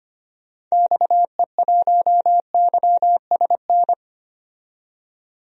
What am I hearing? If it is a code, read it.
Morse code, heard as a keyed tone, about 25 wpm, 700 Hz. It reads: XE1YSN
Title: DXCC-100 Call Signs - 6 Characters - 25wpm